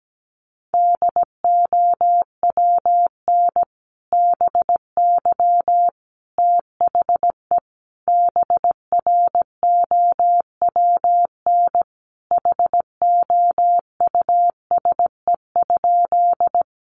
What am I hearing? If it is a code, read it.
Morse code, heard as a keyed tone, about 17 wpm, 700 Hz. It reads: DOWN BY THE BROWN HOUSE?
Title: Sentences from Top 1200 Words 17wpm